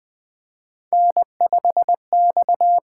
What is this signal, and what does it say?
Morse code, heard as a keyed tone, about 20 wpm, 700 Hz. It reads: N5X
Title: USA Call Signs - 8x Character Spacing - 20wpm